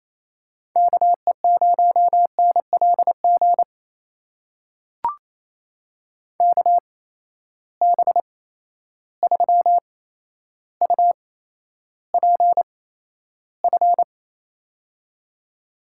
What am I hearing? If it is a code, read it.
Morse code, heard as a keyed tone, about 28 wpm, spaced as for 12 wpm, 700 Hz. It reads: KE0NLG KB3UPF
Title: USA Call Signs - 6 Characters - 8x Character Spacing - 28wpm